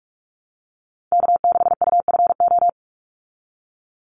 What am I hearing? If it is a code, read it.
Morse code, heard as a keyed tone, about 44 wpm, 700 Hz. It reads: K6UFO
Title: Call Signs Repeated 3x - 45wpm